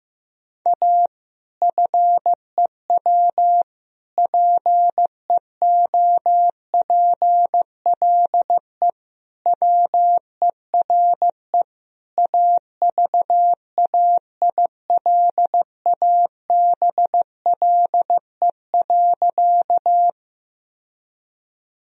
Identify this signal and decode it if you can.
Morse code, heard as a keyed tone, about 15 wpm, 700 Hz. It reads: A FEW PEOPLE WERE AVAILABLE.
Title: Sentences from Top 1000 Words - Review Part 3 - 15wpm